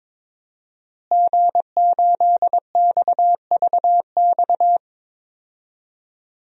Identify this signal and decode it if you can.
Morse code, heard as a keyed tone, about 22 wpm, 700 Hz. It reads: G8XVX
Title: DXCC-100 Call Signs - 5 Characters - Repeated 2x - 22wpm